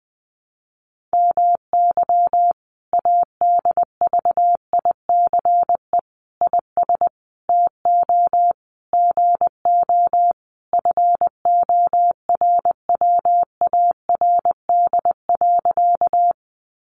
Morse code, heard as a keyed tone, about 20 wpm, 700 Hz. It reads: MY ADVICE IS TO GO FORWARD.
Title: Sentences from Top 1300 Words 20wpm